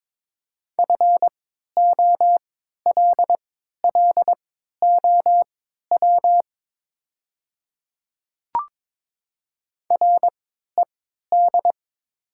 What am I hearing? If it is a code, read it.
Morse code, heard as a keyed tone, about 22 wpm, spaced as for 13 wpm, 700 Hz. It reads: FOLLOW RED
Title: 3x Character Spacing - Top 500 Words - 22wpm